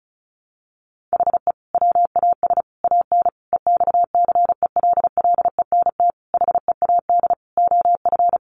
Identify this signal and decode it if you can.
Morse code, heard as a keyed tone, about 34 wpm, 700 Hz. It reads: HE WAS AN EXCELLENT HEAD OF
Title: Sentences from Top 1200 Words 35wpm